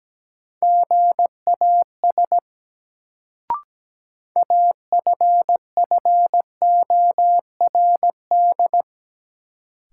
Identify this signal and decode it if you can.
Morse code, heard as a keyed tone, about 17 wpm, 700 Hz. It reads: GAS AFFORD
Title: Sentences from Top 1100 Words - New Word Drill Alt - 17wpm